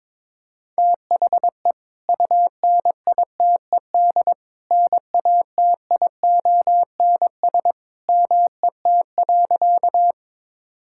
Morse code, heard as a keyed tone, about 22 wpm, 700 Hz. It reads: THE UNITED NATIONS MET.
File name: practice / 22wpm / Sentences with 2-Word Phrases - 22wpm